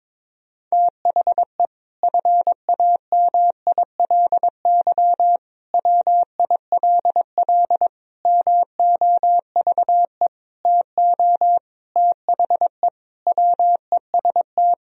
Morse code, heard as a keyed tone, about 22 wpm, 700 Hz. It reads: THE FAMILY WILL MOVE TO THE WEST
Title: Sentences from Top 1000 Words - Review Part 2 - 22wpm